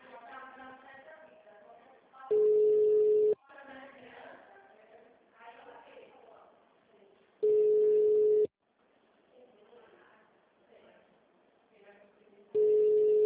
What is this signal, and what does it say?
Ringback tone (Europe, 425 Hz), with speech.
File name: out- -2004- - - .82